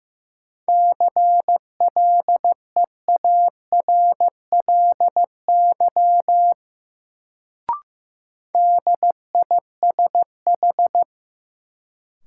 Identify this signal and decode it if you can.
Morse code, heard as a keyed tone, about 15 wpm, 700 Hz. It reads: CLEARLY DISH
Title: Sentences from Top 1300 Words - New Word Drill Alt - 15wpm